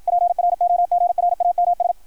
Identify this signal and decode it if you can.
Morse code, heard as a keyed tone, about 53 wpm, 660 Hz. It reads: PROGRAMS
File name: archived / CRSnet